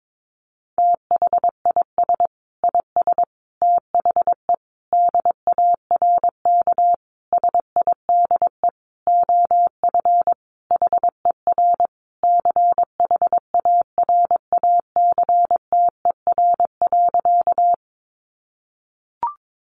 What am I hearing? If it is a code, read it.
Morse code, heard as a keyed tone, about 22 wpm, 700 Hz. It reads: THIS IS THE DARK SIDE OF HER CHARACTER.
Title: Sentences from Top 1200 Words 22wpm